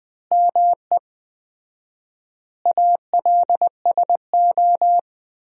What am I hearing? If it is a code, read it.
Morse code, heard as a keyed tone, about 20 wpm, 700 Hz. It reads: ME ALSO
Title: 4x Word-Spacing - Sentences from Top 400 Words - 20wpm